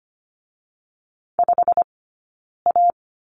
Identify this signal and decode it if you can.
Morse code, heard as a keyed tone, about 25 wpm, spaced as for 9 wpm, 690 Hz. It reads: 5A
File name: Lesson 15d - Number 5 - Two Characters 25wpm